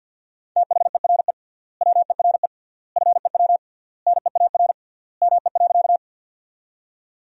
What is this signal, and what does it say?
Morse code, heard as a keyed tone, about 50 wpm, 700 Hz. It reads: THERE WERE FEW NEAR ME.
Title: 3x Word-Spacing - Sentences from Top 300 Words - 50wpm